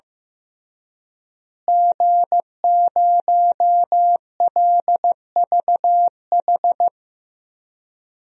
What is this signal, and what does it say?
Morse code, heard as a keyed tone, about 16 wpm, 700 Hz. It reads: G0LVH
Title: DXCC-100 Call Signs - 5 Characters - Repeated 2x - 15wpm